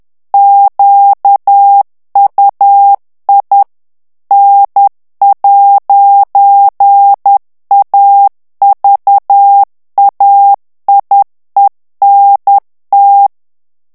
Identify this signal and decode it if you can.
Morse code, heard as a keyed tone, about 10 wpm, 800 Hz. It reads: QUI N'AVAIENT
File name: morse mp3 / FRANCAIS TEXTE CLAIR 2 VITESSE 10 MOTS MINUTES